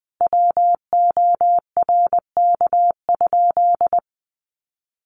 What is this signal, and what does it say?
Morse code, heard as a keyed tone, about 20 wpm, 700 Hz. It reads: WORK?
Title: Sentences from Top 1300 Words 20wpm